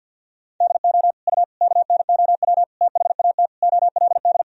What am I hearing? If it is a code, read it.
Morse code, heard as a keyed tone, about 52 wpm, 700 Hz. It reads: DO U KNOW THAT OLD
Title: Sentences from Top 400 Words and Abbreviations 50wpm